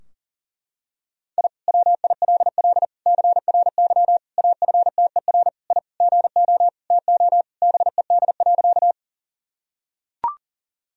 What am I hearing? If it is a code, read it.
Morse code, heard as a keyed tone, about 40 wpm, 700 Hz. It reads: I WILL CRY AFTER I GO TO BED.
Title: Sentences from Top 1000 Words - Review Part 2 - 40wpm